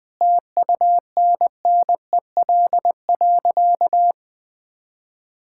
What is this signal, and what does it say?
Morse code, heard as a keyed tone, about 19 wpm, 700 Hz. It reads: TUNNEL.
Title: Sentences with 8-Word Phrases - 20wpm